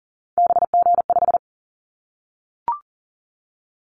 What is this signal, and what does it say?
Morse code, heard as a keyed tone, about 40 wpm, 700 Hz. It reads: BG5
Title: Callsign Prefixes for Top 100 DXCC - 40wpm